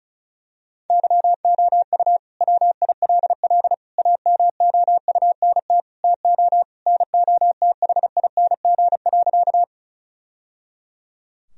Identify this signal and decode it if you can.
Morse code, heard as a keyed tone, about 36 wpm, 700 Hz. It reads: YOU WILL AMOUNT TO NOTHING.
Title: Sentences from Top 1000 Words - Review Part 4 - 35wpm